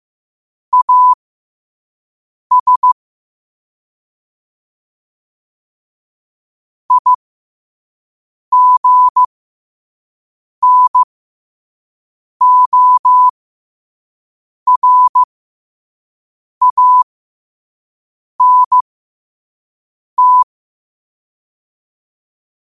Morse code, heard as a keyed tone, about 15 wpm, spaced as for 5 wpm, 1000 Hz. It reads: AS IGNORANT
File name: qotdfiles / 5wpm